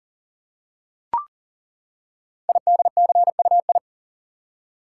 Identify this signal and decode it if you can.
Morse code, heard as a keyed tone, about 38 wpm, 700 Hz. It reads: IDCUI
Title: Code Groups - 5 Letter:Numbers - 40wpm